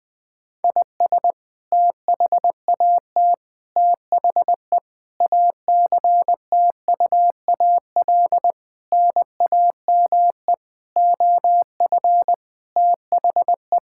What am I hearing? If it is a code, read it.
Morse code, heard as a keyed tone, about 20 wpm, 700 Hz. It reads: IS THAT THE ACTUAL NAME OF THE
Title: Sentences from Top 1000 Words - Review Part 5 - 20wpm